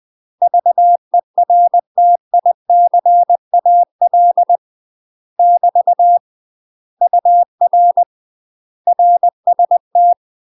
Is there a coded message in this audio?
Yes: Morse code from 0.4 to 10.1 s, heard as a keyed tone, about 20 wpm, 700 Hz. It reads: VERTICAL <BT> UR RST